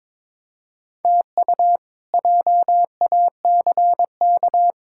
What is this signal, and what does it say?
Morse code, heard as a keyed tone, about 23 wpm, 700 Hz. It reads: TU JACK
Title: K1USN SST QSO Practice - 22wpm